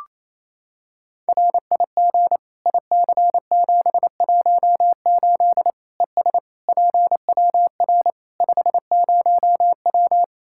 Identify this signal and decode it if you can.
Morse code, heard as a keyed tone, about 28 wpm, 700 Hz. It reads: RIG IC718 ES PWR 50W
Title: Practice QSO - Exchange 2 - Info - 28wpm